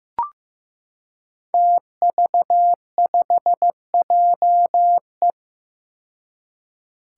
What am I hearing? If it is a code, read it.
Morse code, heard as a keyed tone, about 15 wpm, 700 Hz. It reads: TV5JE
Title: Five Challenging Letter-Number Combinations 15wpm